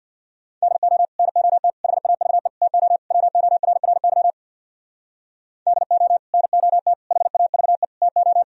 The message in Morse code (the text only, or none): DO NOT HAVE TO WORRY DO NOT HAVE TO